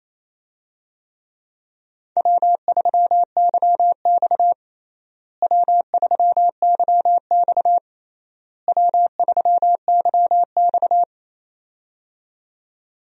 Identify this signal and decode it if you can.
Morse code, heard as a keyed tone, about 28 wpm, 700 Hz. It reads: W3YX W3YX W3YX
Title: USA Call Signs - 4 Characters - Repeated 3x - 28wpm